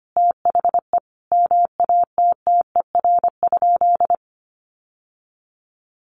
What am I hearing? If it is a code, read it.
Morse code, heard as a keyed tone, about 25 wpm, 700 Hz. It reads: THE MATTER?
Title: Sentences from Top 1100 Words 25wpm